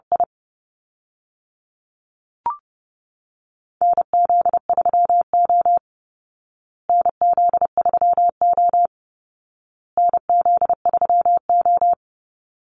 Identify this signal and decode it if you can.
Morse code, heard as a keyed tone, about 28 wpm, 700 Hz. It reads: I NZ3O NZ3O NZ3O
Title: Call Signs Repeated 2x - 30wpm